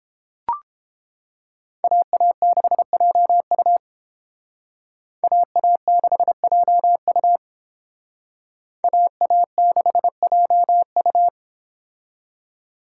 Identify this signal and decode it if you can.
Morse code, heard as a keyed tone, about 29 wpm, 700 Hz. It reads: AA6JU AA6JU AA6JU